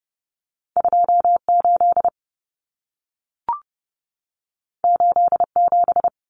Morse code, heard as a keyed tone, about 30 wpm, 700 Hz. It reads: 28 87
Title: Common Cut Numbers - 2-Digit 30wpm